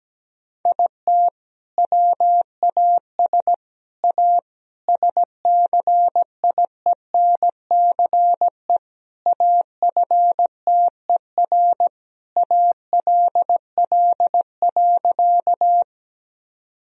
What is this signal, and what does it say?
Morse code, heard as a keyed tone, about 17 wpm, 700 Hz. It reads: IT WAS A SCIENCE AFTER ALL.
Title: Sentences from Top 1000 Words - Review Part 2 - 17wpm